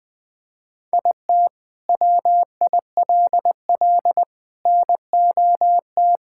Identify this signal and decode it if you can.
Morse code, heard as a keyed tone, about 21 wpm, 700 Hz. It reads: IT WILL NOT
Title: Sentences from Top 1000 Words - Review Part 6 - 20wpm